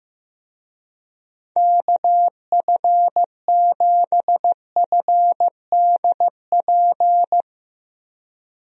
Morse code, heard as a keyed tone, about 15 wpm, 700 Hz. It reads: KF7FDP
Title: USA Call Signs - 6 Characters - Repeated 3x - 15wpm